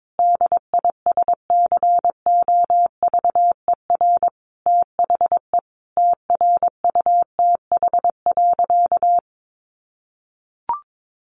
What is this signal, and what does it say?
Morse code, heard as a keyed tone, about 22 wpm, 700 Hz. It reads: DISCOVER THE TRUTH.
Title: Sentences from Top 1100 Words 22wpm